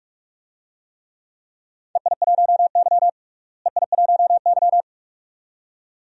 Morse code, heard as a keyed tone, about 42 wpm, 700 Hz. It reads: EI1Y EI1Y
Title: DXCC-100 Call Signs - 4 Characters - Repeated 2x - 45wpm